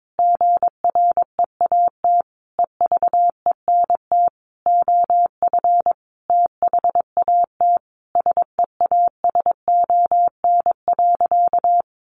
Morse code, heard as a keyed tone, about 22 wpm, 700 Hz. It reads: GREAT EVENT OF THAT SEASON.
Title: Sentences from Top 1400 Words 22wpm